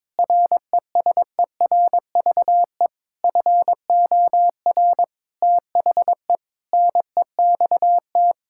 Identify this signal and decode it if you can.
Morse code, heard as a keyed tone, about 22 wpm, 700 Hz. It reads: RESERVE FOR THE NEXT
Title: Sentences from Top 1000 Words - Review Part 6 - 22wpm